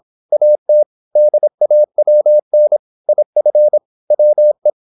Morse code, heard as a keyed tone, about 26 wpm, 600 Hz. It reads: AT DAWN IF WE
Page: Never 26wpm | CW med Gnister